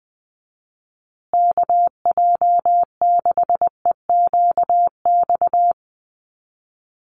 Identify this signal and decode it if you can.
Morse code, heard as a keyed tone, about 20 wpm, 700 Hz. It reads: KJ6EQX
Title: USA Call Signs Repeated 2x 20wpm